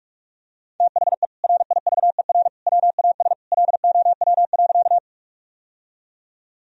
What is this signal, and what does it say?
Morse code, heard as a keyed tone, about 41 wpm, 700 Hz. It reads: THE RIVER WAS LOW.
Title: Sentences from Top 1000 Words - Review Part 1 - 45wpm